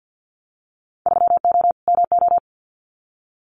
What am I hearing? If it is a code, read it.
Morse code, heard as a keyed tone, about 50 wpm, 700 Hz. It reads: 3O MO
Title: Winter Field Day Exchanges Encore 50wpm